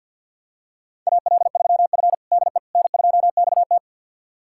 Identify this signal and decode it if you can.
Morse code, heard as a keyed tone, about 52 wpm, 700 Hz. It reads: AL3F DE N2XT
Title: USA Callsign DE Callsign - 4 Characters - 50wpm